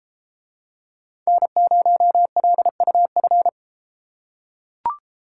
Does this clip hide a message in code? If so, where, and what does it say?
Yes: Morse code from 1.3 to 3.5 s, heard as a keyed tone, about 33 wpm, 700 Hz. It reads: N0LUF